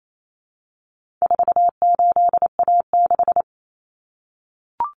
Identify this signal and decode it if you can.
Morse code, heard as a keyed tone, about 28 wpm, 700 Hz. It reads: 48A6
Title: Common Cut Numbers - 4-Digit 28wpm